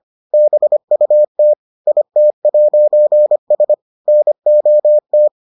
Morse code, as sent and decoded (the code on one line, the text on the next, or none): -... ..- - / .. - .----. ... / -. --- -
BUT IT'S NOT